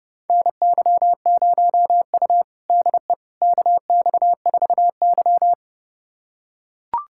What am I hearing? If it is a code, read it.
Morse code, heard as a keyed tone, about 30 wpm, 700 Hz. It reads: NY0U DE KX4Y
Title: USA Callsign DE Callsign - 4 Characters - 30wpm